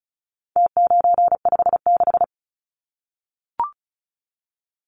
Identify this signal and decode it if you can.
Morse code, heard as a keyed tone, about 34 wpm, 700 Hz. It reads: T956
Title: Numbers - Contest Style - 4-Digit 35wpm